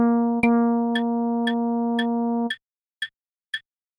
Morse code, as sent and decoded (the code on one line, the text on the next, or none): none